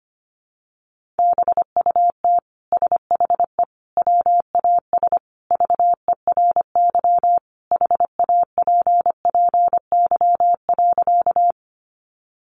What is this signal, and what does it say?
Morse code, heard as a keyed tone, about 25 wpm, 700 Hz. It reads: BUT SHE WAS VERY HAPPY.
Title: Sentences from Top 1000 Words 25wpm